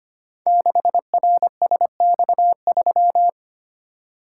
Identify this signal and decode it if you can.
Morse code, heard as a keyed tone, about 25 wpm, 700 Hz. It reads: 6RSX3
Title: Code Groups - 5 Letter:Numbers - 25wpm